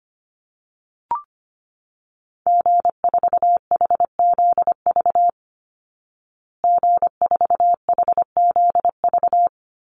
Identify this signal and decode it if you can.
Morse code, heard as a keyed tone, about 25 wpm, 700 Hz. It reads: G4HZV G4HZV
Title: Call Signs Repeated 3x - 25wpm